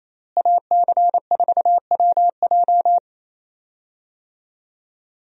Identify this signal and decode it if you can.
Morse code, heard as a keyed tone, about 28 wpm, 700 Hz. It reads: AC4WJ
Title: USA Call Signs - 5 Characters - Repeated 2x - 28wpm